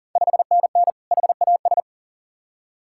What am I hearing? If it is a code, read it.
Morse code, heard as a keyed tone, about 39 wpm, 700 Hz. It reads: 5NN HAS
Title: Ontario QSO Party Exchange 40wpm